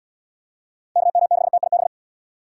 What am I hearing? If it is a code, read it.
Morse code, heard as a keyed tone, about 78 wpm, 700 Hz. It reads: KM6TIB